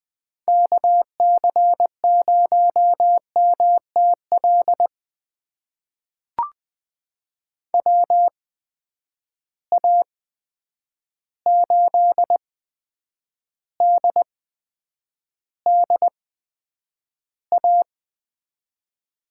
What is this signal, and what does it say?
Morse code, heard as a keyed tone, about 20 wpm, spaced as for 9 wpm, 700 Hz. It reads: KC0MTL WA8DDA
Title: USA Call Signs - 6 Characters - 8x Character Spacing - 20wpm